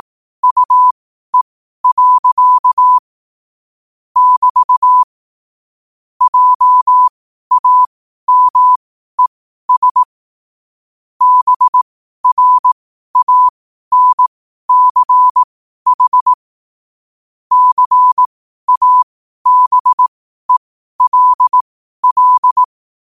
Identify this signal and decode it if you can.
Morse code, heard as a keyed tone, about 18 wpm, spaced as for 12 wpm, 1000 Hz. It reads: UE. <BT> JAMES BRANCH CABELL
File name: qotdfiles / 12wpm